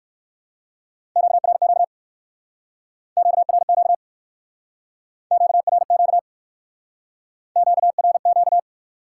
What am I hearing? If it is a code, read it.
Morse code, heard as a keyed tone, about 55 wpm, 700 Hz. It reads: QRQ QRQ QRQ QRQ
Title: Speed-Racing Top 90 QSO Elements 45wpm